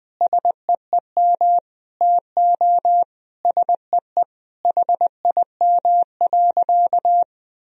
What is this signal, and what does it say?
Morse code, heard as a keyed tone, about 20 wpm, 700 Hz. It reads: SEEM TO SEE HIM.
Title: Sentences from Top 1000 Words - Review Part 1 - 20wpm